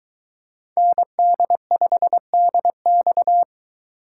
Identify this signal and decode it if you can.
Morse code, heard as a keyed tone, about 22 wpm, 700 Hz. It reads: ND5DX